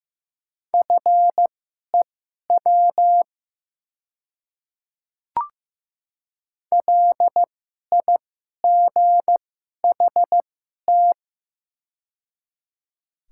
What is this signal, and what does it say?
Morse code, heard as a keyed tone, about 15 wpm, spaced as for 11 wpm, 700 Hz. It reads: FEW LIGHT
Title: 2x Character Spacing - Top 300 Words - 15wpm